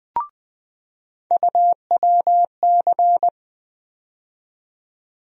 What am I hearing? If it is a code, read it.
Morse code, heard as a keyed tone, about 20 wpm, 700 Hz. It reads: UWC
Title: USA Call Sign Postfixes - 3 Letters 20wpm